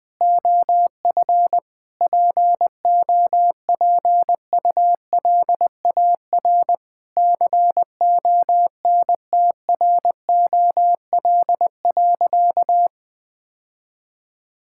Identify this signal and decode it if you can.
Morse code, heard as a keyed tone, about 20 wpm, 700 Hz. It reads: OF POPULAR CONTROL.
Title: Sentences from Top 1000 Words - Review Part 5 - 20wpm